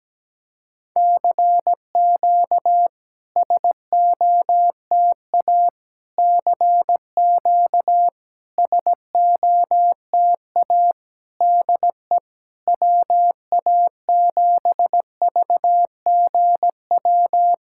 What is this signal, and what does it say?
Morse code, heard as a keyed tone, about 17 wpm, 700 Hz. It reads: CQ SOTA CQ SOTA DE WA7VGW
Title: SOTA QSO Practice 17wpm